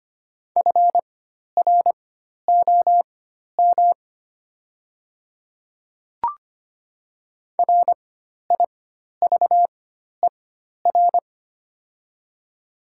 Morse code, heard as a keyed tone, about 25 wpm, spaced as for 12 wpm, 690 Hz. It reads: FROM RIVER
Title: 4x Character Spacing - Top 300 Words - 25wpm